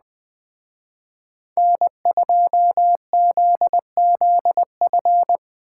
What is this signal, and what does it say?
Morse code, heard as a keyed tone, about 20 wpm, 700 Hz. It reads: N2ZZF